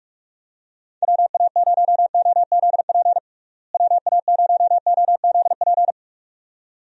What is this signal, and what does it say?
Morse code, heard as a keyed tone, about 46 wpm, 700 Hz. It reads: WA0OZP WA0OZP
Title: USA Call Signs - 6 Characters - Repeated 2x - 45wpm